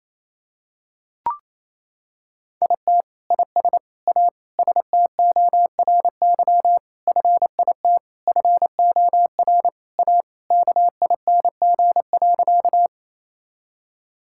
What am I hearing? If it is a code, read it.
Morse code, heard as a keyed tone, about 28 wpm, 700 Hz. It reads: IT IS A STORY FIT FOR A KING.
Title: Sentences from Top 1000 Words - Review Part 5 - 28wpm